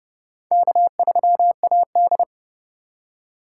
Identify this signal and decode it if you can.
Morse code, heard as a keyed tone, about 28 wpm, 700 Hz. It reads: K3AD